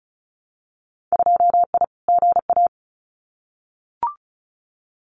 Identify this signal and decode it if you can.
Morse code, heard as a keyed tone, about 34 wpm, 700 Hz. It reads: 2I GA
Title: Winter Field Day Exchanges 35wpm